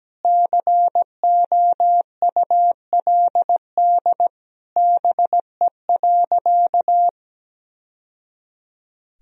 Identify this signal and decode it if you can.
Morse code, heard as a keyed tone, about 17 wpm, 700 Hz. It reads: COULD BE.
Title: Sentences from Top 1000 Words - Review Part 3 - 17wpm